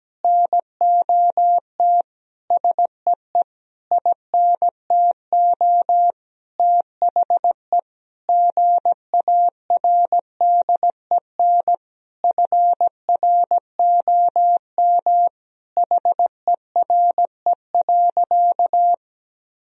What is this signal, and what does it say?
Morse code, heard as a keyed tone, about 17 wpm, 700 Hz. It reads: NOT SEE INTO THE GARDEN FROM HERE.
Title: Sentences from Top 1000 Words - Review Part 4 - 17wpm